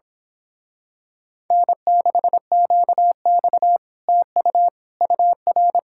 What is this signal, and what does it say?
Morse code, heard as a keyed tone, about 26 wpm, 700 Hz. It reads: N6QX TU UR
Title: Speed-Racing Common QSO Phrases Encore - 22wpm